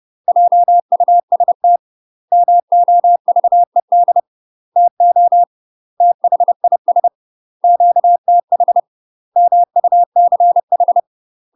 Morse code, heard as a keyed tone, about 30 wpm, 700 Hz. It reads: JUST MOVED TO THIS QTH MUCH
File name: UK MASTER QSOs TRAINING / 30WPM / UK MASTER QSOs 0009